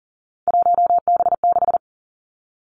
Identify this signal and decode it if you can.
Morse code, heard as a keyed tone, about 41 wpm, 700 Hz. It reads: 1B6
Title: Three Challenging Letter-Number Combinations 40wpm